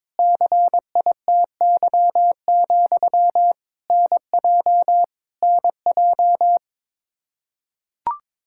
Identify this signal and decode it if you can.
Morse code, heard as a keyed tone, about 21 wpm, 700 Hz. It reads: CITY, NJ NJ
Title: Practice QSO - Exchange 1 - Beginning - 22wpm